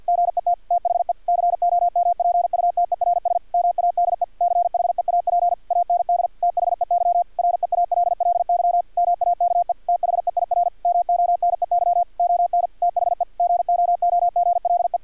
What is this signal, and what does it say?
Morse code, heard as a keyed tone, about 50 wpm, 700 Hz, with radio noise. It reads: GET THE COMPUTERS MADE CHEAP AND THEY REALLY MAKE THEIR MONEY ON THE GOOGLE
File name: archived / CRSnet